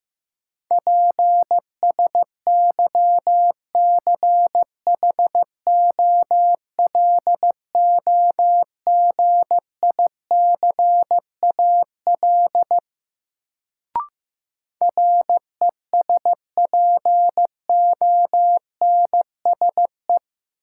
Morse code, heard as a keyed tone, about 15 wpm, 700 Hz. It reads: PSYCHOLOGICAL RESPONSE
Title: Sentences from Top 1400 Words - New Word Drill - 15wpm